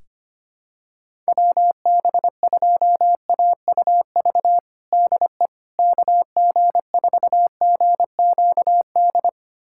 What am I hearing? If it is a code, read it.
Morse code, heard as a keyed tone, about 25 wpm, 700 Hz. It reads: WB2AUV DE KG4GQD
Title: USA Callsign DE Callsign - 6 Characters - 25wpm